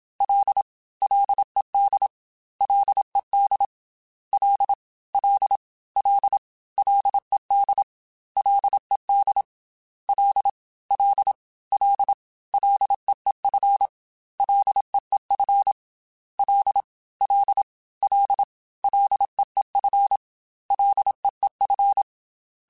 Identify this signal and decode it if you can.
Morse code, heard as a keyed tone, about 27 wpm, 800 Hz. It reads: L LED LED L L L LED LED L L L LEEF LEEF L L L LEEF LEEF
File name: Ls / Single L m3s / Single L Prefixe 3x2 Twice @25